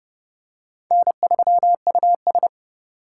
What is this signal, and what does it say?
Morse code, heard as a keyed tone, about 31 wpm, 700 Hz. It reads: N3US